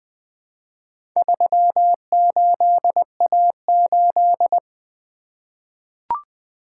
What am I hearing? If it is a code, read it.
Morse code, heard as a keyed tone, about 20 wpm, 700 Hz. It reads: 38A8
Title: Four Letter-Number Combinations - 20wpm